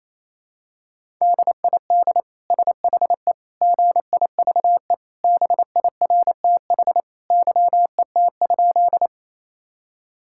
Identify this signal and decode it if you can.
Morse code, heard as a keyed tone, about 28 wpm, 700 Hz. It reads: DID SHE GIVE BIRTH YET?
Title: Sentences from Top 1000 Words - Review Part 5 - 28wpm